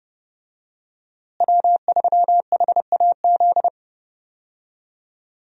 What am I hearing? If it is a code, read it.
Morse code, heard as a keyed tone, about 30 wpm, 700 Hz. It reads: W3HAZ
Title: USA Call Signs - 5 Characters - Repeated 3x - 30wpm